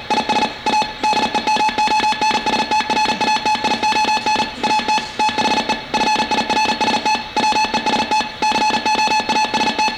Morse code, heard as a keyed tone, about 38 wpm, 840 Hz. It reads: IS A DEMONSTRATION AT THE FIRST WEST COAST